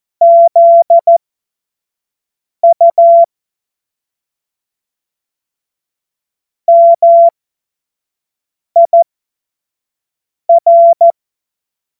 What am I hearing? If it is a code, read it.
Morse code, heard as a keyed tone, about 14 wpm, spaced as for 5 wpm, 680 Hz. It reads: ZU MIR
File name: Morsekurs / Witze / Witz 11. Januar 2022, 25 F70